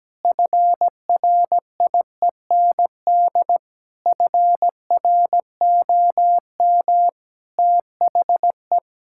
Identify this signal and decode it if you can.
Morse code, heard as a keyed tone, about 17 wpm, 700 Hz. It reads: FRIEND FROM THE
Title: Sentences from Top 1000 Words - Review Part 5 - 17wpm